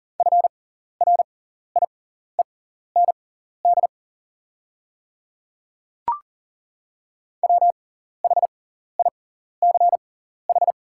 Morse code, heard as a keyed tone, about 40 wpm, spaced as for 14 wpm, 690 Hz. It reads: FRIEND WHICH
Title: 6x Character Spacing - Top 300 Words - 40wpm